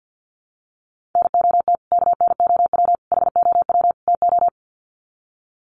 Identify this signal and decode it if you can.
Morse code, heard as a keyed tone, about 50 wpm, 700 Hz. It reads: NOT KNOW HOW TO
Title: Sets of 4 Words 50wpm